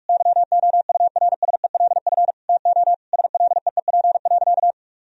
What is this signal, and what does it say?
Morse code, heard as a keyed tone, about 46 wpm, 700 Hz. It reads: YOURSELF TO SLEEP.
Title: Sentences from Top 1000 Words - Review Part 2 - 45wpm